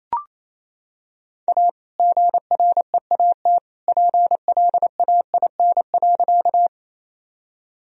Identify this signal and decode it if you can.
Morse code, heard as a keyed tone, about 28 wpm, 700 Hz. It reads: A GREAT PLAIN.
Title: Sentences from Top 1000 Words - Review Part 2 - 28wpm